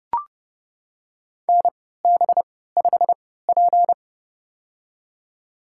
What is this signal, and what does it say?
Morse code, heard as a keyed tone, about 30 wpm, spaced as for 17 wpm, 700 Hz. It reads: NB5P
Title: USA Call Signs - 4 Characters - 3x Character Spacing - 30wpm